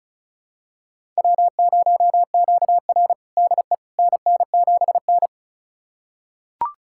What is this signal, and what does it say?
Morse code, heard as a keyed tone, about 34 wpm, 700 Hz. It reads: W0QR DE NN7N
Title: USA Callsign DE Callsign - 4 Characters - 35wpm